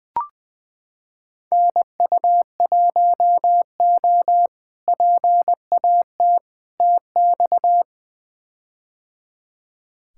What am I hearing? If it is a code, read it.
Morse code, heard as a keyed tone, about 20 wpm, 700 Hz. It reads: NU1O PAT TX
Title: K1USN SST QSO Practice - 20wpm